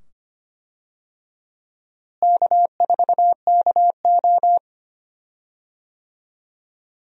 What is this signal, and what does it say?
Morse code, heard as a keyed tone, about 25 wpm, 700 Hz. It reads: K4KO
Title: USA Call Signs - 4 Characters - Repeated 2x - 25wpm